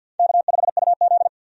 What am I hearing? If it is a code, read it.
Morse code, heard as a keyed tone, about 52 wpm, 700 Hz. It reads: K5UZ
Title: USA Call Signs - 4 Characters - 50wpm